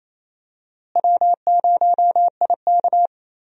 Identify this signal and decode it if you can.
Morse code, heard as a keyed tone, about 28 wpm, 700 Hz. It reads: W0IK
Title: USA Call Signs - 4 Characters - Repeated 3x - 28wpm